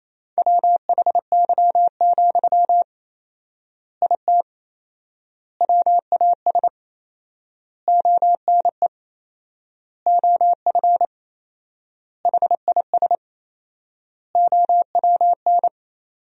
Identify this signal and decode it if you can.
Morse code, heard as a keyed tone, about 28 wpm, 700 Hz. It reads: WHY, IT WAS ONE OF HIS OWN
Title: 4x Word-Spacing - Sentences from Top 300 Words - 28wpm